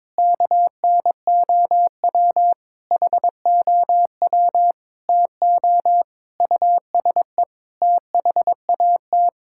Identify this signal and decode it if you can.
Morse code, heard as a keyed tone, about 22 wpm, 700 Hz. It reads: KNOW HOW TO USE THAT
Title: Sentences from Top 1000 Words - Review Part 6 - 22wpm